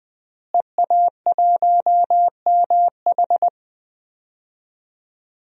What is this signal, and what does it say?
Morse code, heard as a keyed tone, about 20 wpm, 700 Hz. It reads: EA1MH
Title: DXCC-100 Call Signs - 5 Characters - Repeated 3x - 20wpm